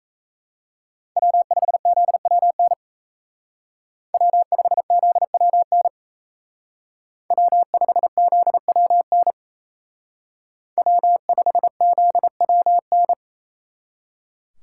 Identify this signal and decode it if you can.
Morse code, heard as a keyed tone, about 35 wpm, 700 Hz. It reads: W5ZWN W5ZWN W5ZWN W5ZWN